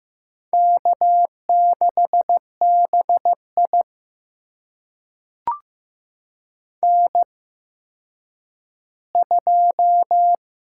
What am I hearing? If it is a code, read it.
Morse code, heard as a keyed tone, about 16 wpm, spaced as for 9 wpm, 700 Hz. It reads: K6BI N2
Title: USA Call Signs - 4 Characters - 8x Character Spacing - 15wpm